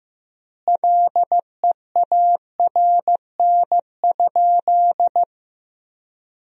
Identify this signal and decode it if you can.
Morse code, heard as a keyed tone, about 16 wpm, 700 Hz. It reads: LEARN?
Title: 3x Word-Spacing - Sentences from Top 300 Words - 15wpm